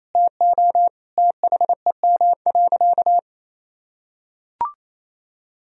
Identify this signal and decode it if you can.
Morse code, heard as a keyed tone, about 28 wpm, 700 Hz. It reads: TO THEM.
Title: Sentences from Top 1000 Words - Review Part 3 - 28wpm